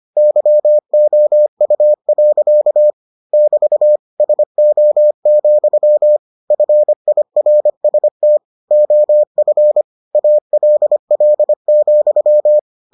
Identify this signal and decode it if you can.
Morse code, heard as a keyed tone, about 25 wpm, 600 Hz. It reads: YOU. <BT> SO, FIRST OF ALL,